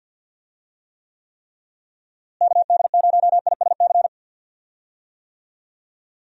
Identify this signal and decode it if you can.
Morse code, heard as a keyed tone, about 52 wpm, 700 Hz. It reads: KD0ISC
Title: USA Call Signs - 6 Characters - Repeated 3x - 50wpm